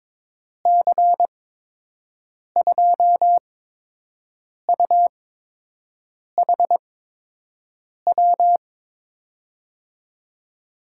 Morse code, heard as a keyed tone, about 22 wpm, spaced as for 6 wpm, 700 Hz. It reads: C2UHW
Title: USA Call Signs - 8x Character Spacing - 22wpm